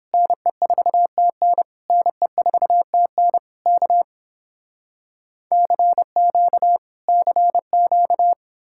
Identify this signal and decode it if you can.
Morse code, heard as a keyed tone, about 32 wpm, 700 Hz. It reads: NE4TN NE4TN K CQ CQ